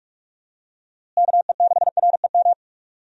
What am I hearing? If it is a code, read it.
Morse code, heard as a keyed tone, about 45 wpm, 700 Hz. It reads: KE6REM
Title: USA Call Signs - 6 Characters - Repeated 2x - 45wpm